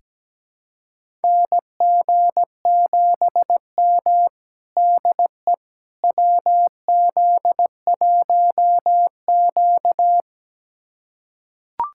Morse code, heard as a keyed tone, about 17 wpm, 700 Hz. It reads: NG7M DE WZ1Q
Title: USA Callsign DE Callsign - 4 Characters - 17wpm